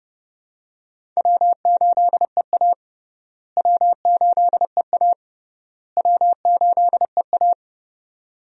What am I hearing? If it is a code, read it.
Morse code, heard as a keyed tone, about 30 wpm, 700 Hz. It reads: W8EA W8EA W8EA
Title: USA Call Signs - 4 Characters - Repeated 3x - 30wpm